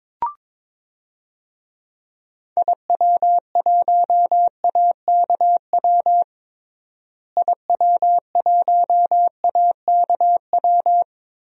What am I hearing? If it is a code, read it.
Morse code, heard as a keyed tone, about 22 wpm, 700 Hz. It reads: IW1AKW IW1AKW
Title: DXCC-100 Call Signs - 6 Characters - Repeated 3x - 22wpm